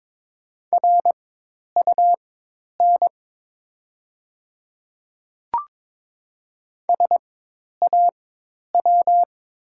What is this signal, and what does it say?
Morse code, heard as a keyed tone, about 22 wpm, spaced as for 10 wpm, 700 Hz. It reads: RUN SAW
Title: 4x Character Spacing - Top 300 Words - 22wpm